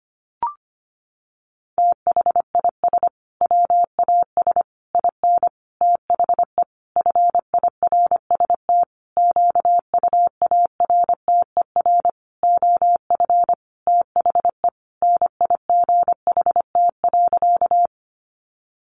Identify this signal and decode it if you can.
Morse code, heard as a keyed tone, about 25 wpm, 700 Hz. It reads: THIS WAS IN THE FIRST QUARTER OF THE NIGHT.
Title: Sentences from Top 1300 Words 25wpm